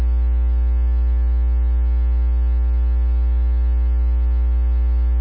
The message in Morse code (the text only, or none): none